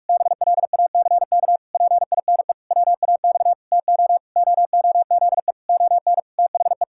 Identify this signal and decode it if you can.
Morse code, heard as a keyed tone, about 46 wpm, 700 Hz. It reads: BLACK PINE WAX TO OOZE ON THE
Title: Pangram Challenge v2 - 45wpm